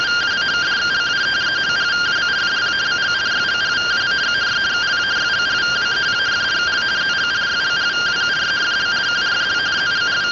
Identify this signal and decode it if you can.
Two-tone teleprinter signal, about 170 Hz shift.